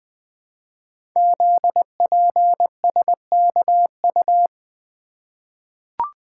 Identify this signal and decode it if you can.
Morse code, heard as a keyed tone, about 20 wpm, 700 Hz. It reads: ZPSKU
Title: Code Groups - 5 Letter:Numbers - 20wpm